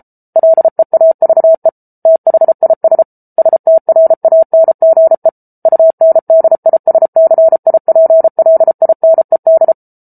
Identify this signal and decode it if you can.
Morse code, heard as a keyed tone, about 34 wpm, 650 Hz. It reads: LEAVE THIS STRANGE UNDISCIPLINED